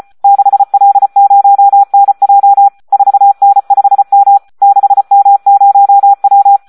Morse code, heard as a keyed tone, about 34 wpm, 800 Hz. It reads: 6L0NJ 4N5M 6M0W
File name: Audio practica CW / SD 1 34